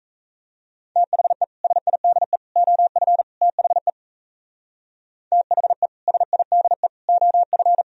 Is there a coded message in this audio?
Yes: Morse code from 1.0 to 7.8 s, heard as a keyed tone, about 37 wpm, 700 Hz. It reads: THE SIDE OF THE THE SIDE OF